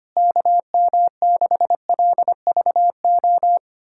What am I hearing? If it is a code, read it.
Morse code, heard as a keyed tone, about 25 wpm, 700 Hz. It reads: KM6LVO
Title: USA Callsign DE Callsign 25wpm